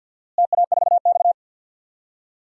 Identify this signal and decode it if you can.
Morse code, heard as a keyed tone, about 49 wpm, 700 Hz. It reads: TA4X
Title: DXCC-100 Call Signs - 4 Characters - Repeated 3x - 50wpm